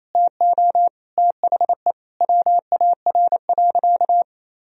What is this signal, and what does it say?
Morse code, heard as a keyed tone, about 28 wpm, 700 Hz. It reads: TO THE WAR.
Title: Sentences from Top 400 Words and Abbreviations 28wpm